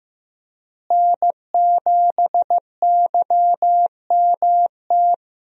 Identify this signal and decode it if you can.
Morse code, heard as a keyed tone, about 15 wpm, 700 Hz. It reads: N7YMT
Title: USA Call Signs - 5 Characters - Repeated 2x - 15wpm